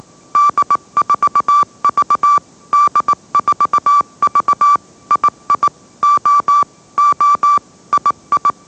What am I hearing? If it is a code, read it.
Morse code, heard as a keyed tone, about 18 wpm, 1160 Hz. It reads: D4V D4V II O O II